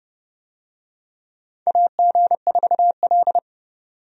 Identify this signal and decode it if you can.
Morse code, heard as a keyed tone, about 32 wpm, 700 Hz. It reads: AG4L
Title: USA Call Signs - 4 Characters - 30wpm